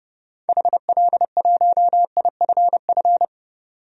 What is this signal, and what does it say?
Morse code, heard as a keyed tone, about 30 wpm, 700 Hz. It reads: HL1IFF
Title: DXCC-100 Call Signs - 6 Characters - 30wpm